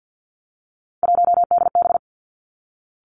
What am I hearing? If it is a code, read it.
Morse code, heard as a keyed tone, about 50 wpm, 700 Hz. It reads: 1DB
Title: Three Challenging Letter-Number Combinations 50wpm